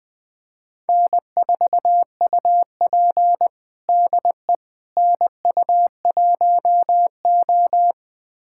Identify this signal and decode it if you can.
Morse code, heard as a keyed tone, about 20 wpm, 700 Hz. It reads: N4UP DE NU1O
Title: USA Callsign DE Callsign - 4 Characters - 20wpm